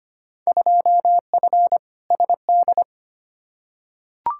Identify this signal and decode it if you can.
Morse code, heard as a keyed tone, about 25 wpm, 700 Hz. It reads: 2F SD
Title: Field Day Exchanges Encore - 25wpm